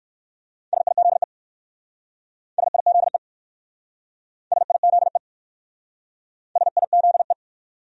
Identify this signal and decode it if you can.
Morse code, heard as a keyed tone, about 60 wpm, 700 Hz. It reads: SIZE SIZE SIZE SIZE